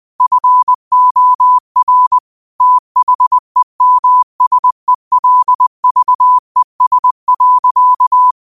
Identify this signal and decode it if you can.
Morse code, heard as a keyed tone, about 20 wpm, 1000 Hz. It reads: FOR THEMSELVES.
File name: qotdfiles / 20wpm